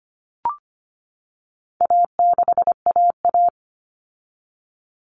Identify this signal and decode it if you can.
Morse code, heard as a keyed tone, about 25 wpm, 700 Hz. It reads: A6AA
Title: All Cut Numbers - 4-Digit 25wpm